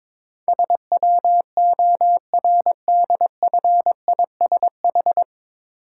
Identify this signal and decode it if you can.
Morse code, heard as a keyed tone, about 22 wpm, 700 Hz. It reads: SWORDFISH
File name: Difficult Words with Hints - 22wpm